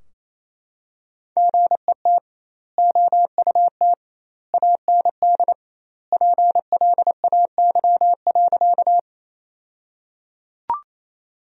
Morse code, heard as a keyed tone, about 27 wpm, 700 Hz. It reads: GET OUT AND PLAY.
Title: 2x Word-Spacing - Sentences from Top 200 Words - 28wpm